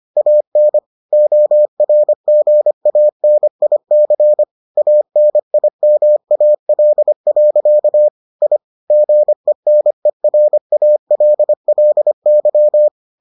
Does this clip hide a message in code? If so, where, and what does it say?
Yes: Morse code from 0.2 to 12.9 s, heard as a keyed tone, about 25 wpm, 600 Hz. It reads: AN ORGANIC ANIMAL. I GENERALLY